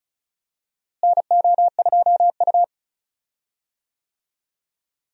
Morse code, heard as a keyed tone, about 36 wpm, 700 Hz. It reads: NO2U
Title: USA Call Signs - 4 Characters - Repeated 3x - 35wpm